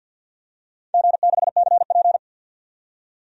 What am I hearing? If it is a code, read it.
Morse code, heard as a keyed tone, about 52 wpm, 700 Hz. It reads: G6CP
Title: DXCC-100 Call Signs - 4 Characters - 50wpm